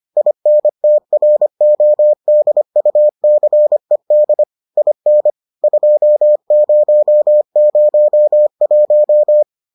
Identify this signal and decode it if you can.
Morse code, heard as a keyed tone, about 25 wpm, 600 Hz. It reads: INTRODUCED IN 2001